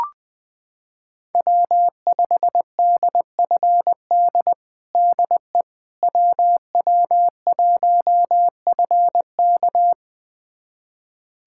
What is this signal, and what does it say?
Morse code, heard as a keyed tone, about 20 wpm, 700 Hz. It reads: W5DFD DE WW1FK
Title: USA Callsign DE Callsign - 5 Characters - 20wpm